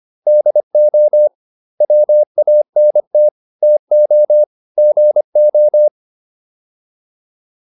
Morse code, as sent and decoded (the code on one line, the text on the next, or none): -.. --- / .-- .- -. - / - --- / --. ---
DO WANT TO GO